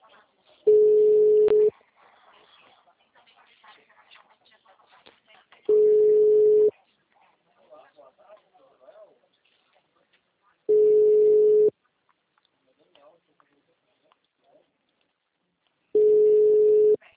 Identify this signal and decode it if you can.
Ringback tone (Europe, 425 Hz), with speech.
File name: out- -2002- - - .60744